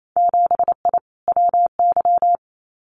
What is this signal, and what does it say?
Morse code, heard as a keyed tone, about 28 wpm, 700 Hz. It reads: 7I WY
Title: Winter Field Day Exchanges Encore 28wpm